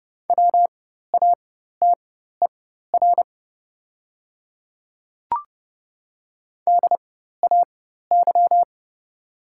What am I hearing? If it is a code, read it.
Morse code, heard as a keyed tone, about 30 wpm, spaced as for 14 wpm, 700 Hz. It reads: WATER DAY
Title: 4x Character Spacing - Top 100 Words - 30wpm